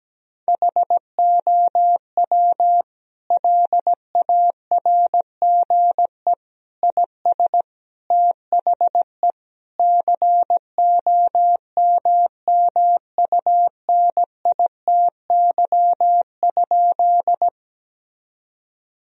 Morse code, heard as a keyed tone, about 17 wpm, 700 Hz. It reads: HOW LARGE IS THE COMMUNITY?
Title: Sentences from Top 1000 Words - Review Part 4 - 17wpm